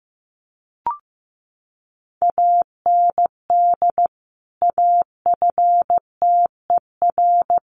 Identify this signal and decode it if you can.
Morse code, heard as a keyed tone, about 15 wpm, 700 Hz. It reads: AND AFTER
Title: Sentences from Top 1400 Words 15wpm